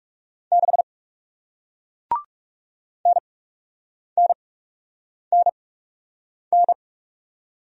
Morse code, heard as a keyed tone, about 41 wpm, spaced as for 8 wpm, 700 Hz. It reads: 6 NNNN